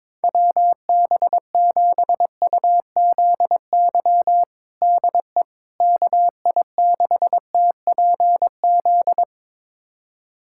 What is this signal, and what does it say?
Morse code, heard as a keyed tone, about 21 wpm, 700 Hz. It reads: WB7UZY DE KI6TPZ
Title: USA Callsign DE Callsign - 6 Characters - 22wpm